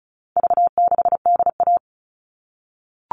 Morse code, heard as a keyed tone, about 34 wpm, 700 Hz. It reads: V6DA
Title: All Cut Numbers - 4-Digit 35wpm